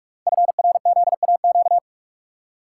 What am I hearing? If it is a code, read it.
Morse code, heard as a keyed tone, about 44 wpm, 700 Hz. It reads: FRZAQ
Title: Code Groups - 5 Letters - 45wpm